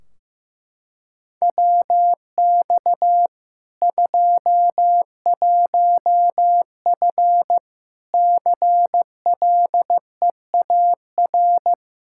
Morse code, heard as a keyed tone, about 15 wpm, 700 Hz. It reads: WX 21F CLEAR
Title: K1USN SST QSO Practice 15wpm